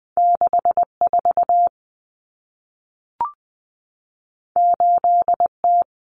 Morse code, heard as a keyed tone, about 20 wpm, 700 Hz. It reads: 64 8T
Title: Common Cut Numbers - 2-Digit 20wpm